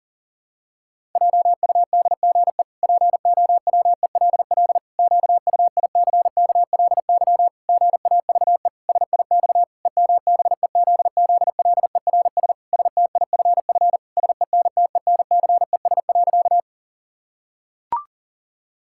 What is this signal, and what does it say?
Morse code, heard as a keyed tone, about 40 wpm, 700 Hz. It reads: JUDGE POWELL QUICKLY GAVE SIX EMBEZZLERS STIFF SENTENCES.
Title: Pangram Challenge v2 - 40wpm